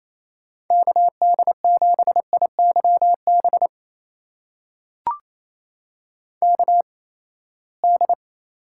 Morse code, heard as a keyed tone, about 28 wpm, spaced as for 19 wpm, 700 Hz. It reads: KD7IYB KD
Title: USA Call Signs - 6 Characters - 8x Character Spacing - 28wpm